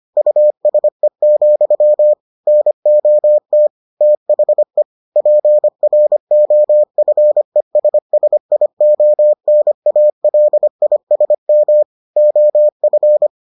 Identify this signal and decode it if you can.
Morse code, heard as a keyed tone, about 25 wpm, 600 Hz. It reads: USE, NOT THE PROFESSIONALISM OF